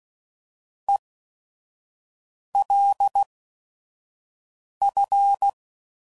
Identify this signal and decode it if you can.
Morse code, heard as a keyed tone, about 16 wpm, spaced as for 5 wpm, 800 Hz. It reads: ELF